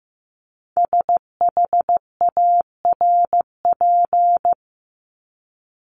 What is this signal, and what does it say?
Morse code, heard as a keyed tone, about 15 wpm, 700 Hz. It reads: SHARP
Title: Top 1501-1750 Words 15wpm